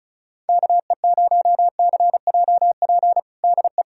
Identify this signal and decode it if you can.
Morse code, heard as a keyed tone, about 36 wpm, 700 Hz. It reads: KE0CJP DE
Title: USA Callsign DE Callsign - 6 Characters - 35wpm